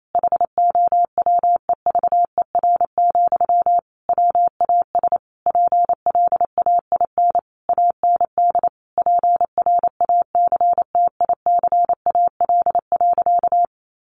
Morse code, heard as a keyed tone, about 28 wpm, 700 Hz. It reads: HOWEVER, WAS PLAIN AND PRACTICAL.
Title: Sentences from Top 1400 Words 28wpm